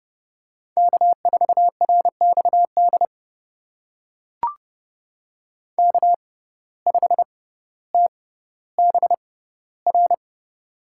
Morse code, heard as a keyed tone, about 31 wpm, spaced as for 15 wpm, 700 Hz. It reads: K4RXD K5TBR
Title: USA Call Signs - 5 Characters - 6x Character Spacing - 30wpm